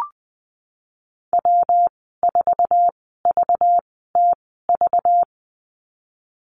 Morse code, heard as a keyed tone, about 20 wpm, spaced as for 15 wpm, 700 Hz. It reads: W4VTV
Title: 207 Lesson 33g - Letter j - Call Signs 20wpm